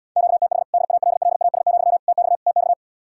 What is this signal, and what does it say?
Morse code, heard as a keyed tone, about 79 wpm, 700 Hz. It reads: QTH DALLAS, TX TX